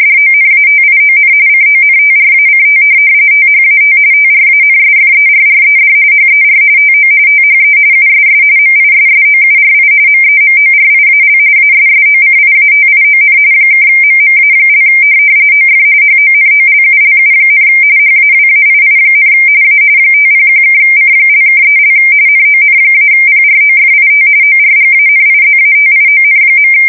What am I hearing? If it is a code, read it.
RTTY, about 45.45 baud. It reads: DONATION CONTRACT THAT TRANSFERRED HER TO THE USS MISSOURI MEMORIAL ASSOCIATION (MMA) OF HONOLULU, HAWAII, A 501(C)(3) NON-PROFIT ORGANIZATION.
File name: fsk170 / USS-Missouri